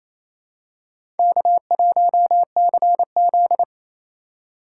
Morse code, heard as a keyed tone, about 28 wpm, 700 Hz. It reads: K1CZ
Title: USA Call Signs - 4 Characters - 2x Character Spacing - 28wpm